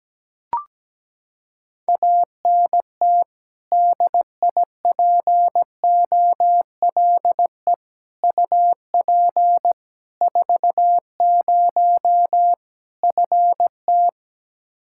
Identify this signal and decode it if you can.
Morse code, heard as a keyed tone, about 17 wpm, 700 Hz. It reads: ANT DIPOLE UP 40 FT
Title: Practice QSO - Exchange 2 - Info - 17wpm